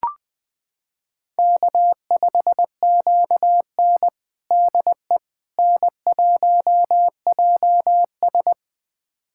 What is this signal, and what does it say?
Morse code, heard as a keyed tone, about 21 wpm, 700 Hz. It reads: K5QN DE N1JS
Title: USA Callsign DE Callsign - 4 Characters - 20wpm